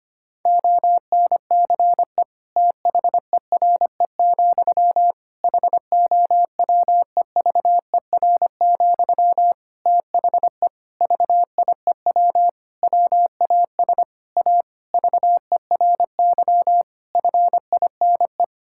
Morse code, heard as a keyed tone, about 25 wpm, 700 Hz. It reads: ONCE THERE, HOWEVER, THE VIEW WAS A VERY FINE
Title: Sentences from Top 1000 Words - Review Part 6 - 25wpm